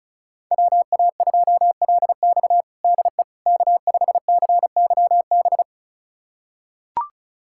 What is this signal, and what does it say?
Morse code, heard as a keyed tone, about 34 wpm, 700 Hz. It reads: WA2LX DE K5CYB
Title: USA Callsign DE Callsign - 5 Characters - 35wpm